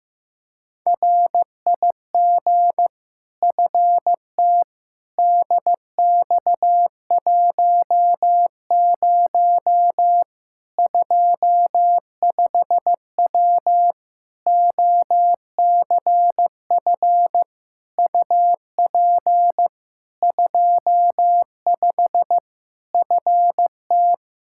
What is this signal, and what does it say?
Morse code, heard as a keyed tone, about 15 wpm, 700 Hz. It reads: RIG FT DX10 25W OCF UP 25 FT
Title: Practice QSO - Exchange 2 - Info - 15wpm